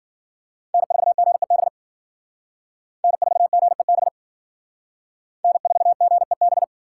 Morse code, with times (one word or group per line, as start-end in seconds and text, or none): 0.7-1.7 N4GEB
3.0-4.1 N4GEB
5.4-6.6 N4GEB